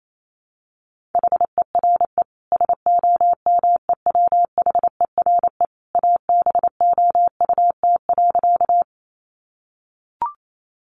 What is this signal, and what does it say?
Morse code, heard as a keyed tone, about 28 wpm, 700 Hz. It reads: HERE SOMEWHERE ABOUT.
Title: Sentences from Top 1400 Words 28wpm